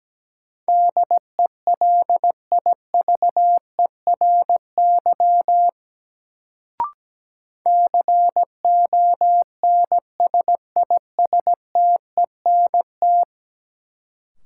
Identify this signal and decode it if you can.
Morse code, heard as a keyed tone, about 17 wpm, 700 Hz. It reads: DELIVERY CONSISTENT
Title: Sentences from Top 1500 Words - New Word Drill Alt - 17wpm